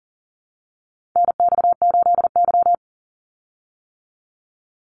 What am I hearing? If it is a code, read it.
Morse code, heard as a keyed tone, about 40 wpm, 700 Hz. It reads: NX8Y
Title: Call Signs - 40wpm